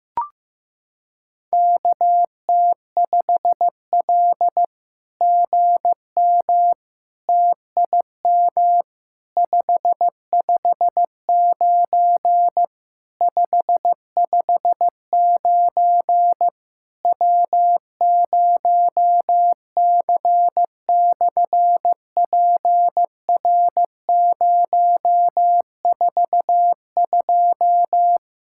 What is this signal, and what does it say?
Morse code, heard as a keyed tone, about 15 wpm, 700 Hz. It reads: KT5L GM TIM 559 559 W0C/PR042
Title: SOTA QSO Practice 15wpm